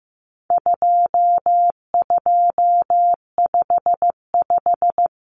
Morse code, heard as a keyed tone, about 15 wpm, 700 Hz. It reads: 2255
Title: Numbers - 4-Digit 15wpm